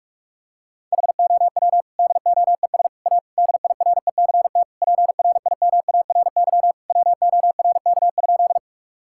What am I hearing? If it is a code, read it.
Morse code, heard as a keyed tone, about 46 wpm, 690 Hz. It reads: HOW DOES A DIRECT PRIMARY WORK?
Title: Sentences from Top 1000 Words - Review Part 5 - 45wpm